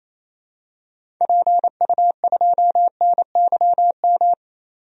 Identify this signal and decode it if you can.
Morse code, heard as a keyed tone, about 28 wpm, 700 Hz. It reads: PU2NYM
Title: DXCC-100 Call Signs - 6 Characters - 28wpm